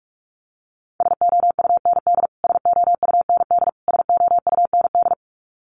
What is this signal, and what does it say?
Morse code, heard as a keyed tone, about 44 wpm, 700 Hz. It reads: SOUND SOUND SOUND
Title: Top 300 Words Repeated 45wpm